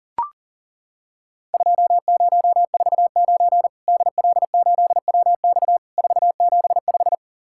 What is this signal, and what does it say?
Morse code, heard as a keyed tone, about 38 wpm, 700 Hz. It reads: 2049 DL8WX 475
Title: Worked All Europe DX Contest - QTC Exchange - 40wpm